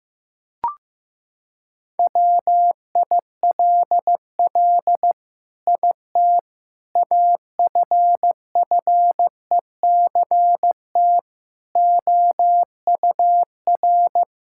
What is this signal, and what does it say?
Morse code, heard as a keyed tone, about 15 wpm, 700 Hz. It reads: WILL IT AFFECT OUR
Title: Sentences from Top 1000 Words - Review Part 6 - 15wpm